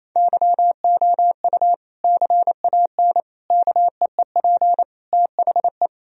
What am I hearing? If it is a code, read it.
Morse code, heard as a keyed tone, about 28 wpm, 700 Hz. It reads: YOU CAN KEEP THE
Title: Sentences from Top 1000 Words - Review Part 1 - 28wpm